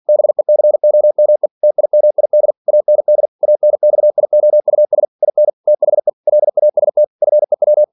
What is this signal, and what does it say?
Morse code, heard as a keyed tone, about 47 wpm, 600 Hz. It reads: BECOME TIMID AND ANXIOUS IN THE LAST FEW
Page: Never 47wpm